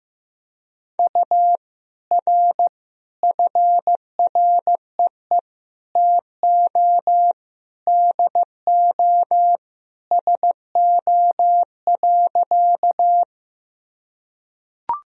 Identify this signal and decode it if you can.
Morse code, heard as a keyed tone, about 15 wpm, 700 Hz. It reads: U R FREE TO DO SO.
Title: Sentences from Top 500 Words and Abbreviations 15wpm